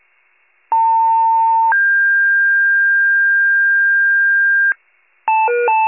Two-tone page, heard as a one-second tone, then a three-second tone, about 900 then 1600 Hz, with noise. It is heard from 0.7 to 4.7 s.